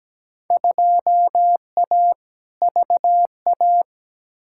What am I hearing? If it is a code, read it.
Morse code, heard as a keyed tone, about 17 wpm, 700 Hz. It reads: 2A VA
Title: Field Day Exchanges - 17wpm